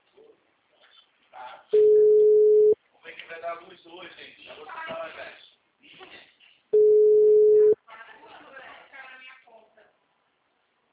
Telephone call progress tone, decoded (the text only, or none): ringback tone